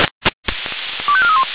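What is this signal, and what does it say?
Five-tone selective call (ZVEI), heard as quick runs of five tones, tones of about 70 ms. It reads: 25431
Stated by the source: ZVEI five-tone selective call 25406 received by SDR.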